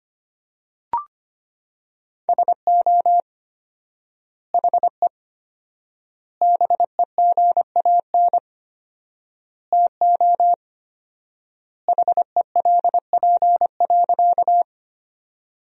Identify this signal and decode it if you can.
Morse code, heard as a keyed tone, about 25 wpm, 700 Hz. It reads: SO HE BEGAN TO HELP.
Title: 4x Word-Spacing - Sentences from Top 300 Words - 25wpm